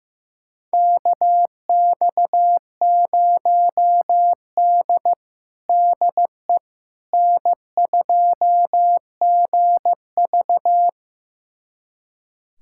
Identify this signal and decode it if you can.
Morse code, heard as a keyed tone, about 15 wpm, 700 Hz. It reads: KX0D DE N2GV
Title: USA Callsign DE Callsign - 4 Characters - 15wpm